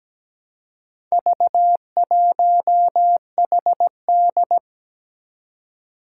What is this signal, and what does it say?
Morse code, heard as a keyed tone, about 17 wpm, 700 Hz. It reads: V1HD
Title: Four Challenging Letter-Number Combinations 17wpm